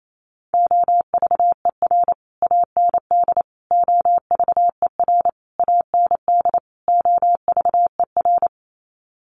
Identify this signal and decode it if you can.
Morse code, heard as a keyed tone, about 28 wpm, 700 Hz. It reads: OVER AND OVER AND OVER
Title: Sets of 5 Words Encore 28wpm